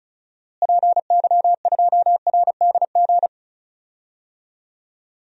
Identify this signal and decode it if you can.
Morse code, heard as a keyed tone, about 36 wpm, 700 Hz. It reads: PY2RDG
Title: DXCC-100 Call Signs - 6 Characters - Repeated 3x - 35wpm